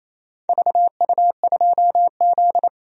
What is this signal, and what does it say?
Morse code, heard as a keyed tone, about 28 wpm, 700 Hz. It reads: VU2Z
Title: DXCC-100 Call Signs - 4 Characters - Repeated 3x - 28wpm